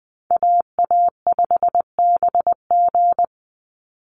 Morse code, heard as a keyed tone, about 20 wpm, 700 Hz. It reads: AA5BG
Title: Call Signs Repeated 3x - 20wpm